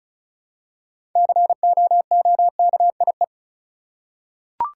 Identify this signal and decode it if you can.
Morse code, heard as a keyed tone, about 36 wpm, 700 Hz. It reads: COOKIE
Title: Six Letter Words - 35wpm